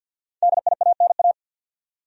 Morse code, heard as a keyed tone, about 50 wpm, 700 Hz. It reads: DIANA
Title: USA Common Names - 50wpm